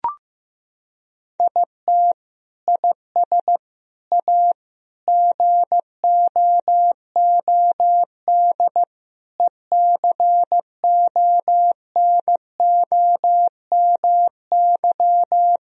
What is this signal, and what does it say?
Morse code, heard as a keyed tone, about 15 wpm, 700 Hz. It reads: IT IS A GOOD ECONOMY
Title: Sentences from Top 1000 Words - Review Part 3 - 15wpm